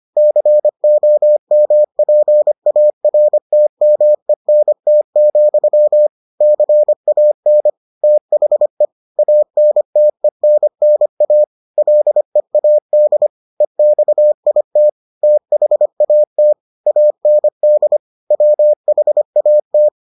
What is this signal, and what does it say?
Morse code, heard as a keyed tone, about 25 wpm, 600 Hz. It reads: COMPARTMENT, CAN THE ANTENNA LEAD EXIT THAT AND WHAT